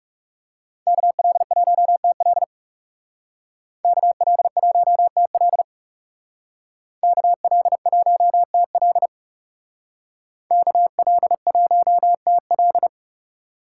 Morse code, heard as a keyed tone, about 36 wpm, 700 Hz. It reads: KL1TL KL1TL KL1TL KL1TL
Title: Speed Racing - USA Call Signs 30wpm